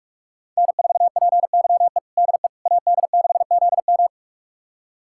Morse code, heard as a keyed tone, about 46 wpm, 700 Hz. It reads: N4PYE DE AD6ZM
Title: USA Callsign DE Callsign - 5 Characters - 45wpm